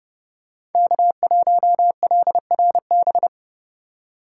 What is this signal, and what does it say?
Morse code, heard as a keyed tone, about 30 wpm, 700 Hz. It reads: K1LRB
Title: USA Call Signs - 5 Characters - 4x Character Spacing - 30wpm